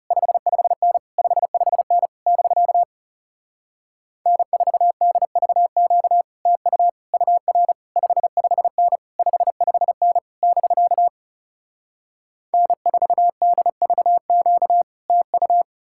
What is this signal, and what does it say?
Morse code, heard as a keyed tone, about 38 wpm, 700 Hz. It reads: 55N 55N <BK> N4DVQ TU UR 55N 55N <BK> N4DVQ TU
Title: Speed-Racing Common QSO Phrases - 30wpm